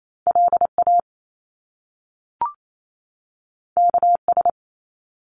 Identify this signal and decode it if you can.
Morse code, heard as a keyed tone, about 28 wpm, 700 Hz. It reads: LA KS
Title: US States 28wpm